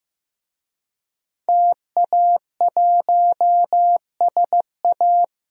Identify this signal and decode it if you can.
Morse code, heard as a keyed tone, about 15 wpm, 700 Hz. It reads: TA1SA
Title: DXCC-100 Call Signs - 5 Characters - Repeated 2x - 15wpm